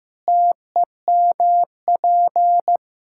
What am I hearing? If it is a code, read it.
Morse code, heard as a keyed tone, about 15 wpm, 700 Hz. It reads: TEMP
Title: K1USN SST QSO Practice 15wpm